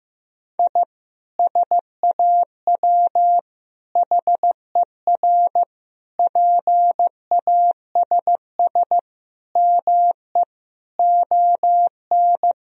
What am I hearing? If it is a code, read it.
Morse code, heard as a keyed tone, about 15 wpm, 700 Hz. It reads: I SAW HER PASS ME ON
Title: Sentences from Top 1000 Words - Review Part 2 - 15wpm